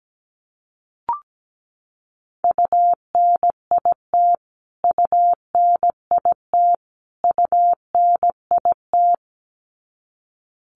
Morse code, heard as a keyed tone, about 17 wpm, 700 Hz. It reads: UNIT UNIT UNIT
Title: Top 500 Words Repeated 17wpm